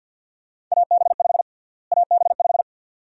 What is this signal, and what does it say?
Morse code, heard as a keyed tone, about 50 wpm, 700 Hz. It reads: AB5 AB5
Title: USA Call Sign Prefixes - Repeated 2x - 50wpm